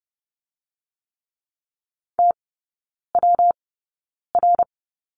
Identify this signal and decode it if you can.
Morse code, heard as a keyed tone, about 30 wpm, 700 Hz. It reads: T W R
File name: Lesson 19e - Letter w - Three Characters 30wpm